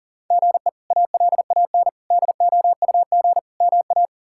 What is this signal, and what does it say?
Morse code, heard as a keyed tone, about 40 wpm, 700 Hz. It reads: GE ALAN DOUG MA
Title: K1USN SST QSO Practice - 40wpm